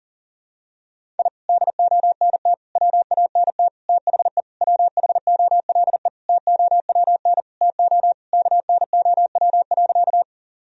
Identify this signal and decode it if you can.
Morse code, heard as a keyed tone, about 40 wpm, 700 Hz. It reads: I DONT WANT THE WHOLE TOWN TO KNOW.
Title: Sentences from Top 1000 Words - Review Part 2 - 40wpm